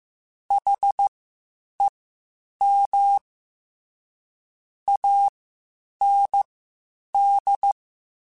Morse code, heard as a keyed tone, about 15 wpm, spaced as for 8 wpm, 800 Hz. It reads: HEM AND
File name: Misc / Misc-mp3s / WordPairs sp @13